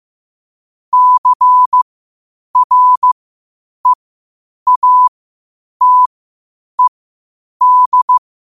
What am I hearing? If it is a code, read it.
Morse code, heard as a keyed tone, about 15 wpm, spaced as for 8 wpm, 1000 Hz. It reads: CREATED
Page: Quotes for Fri, 15 Aug 2025 in Morse Code at 8 words per minute.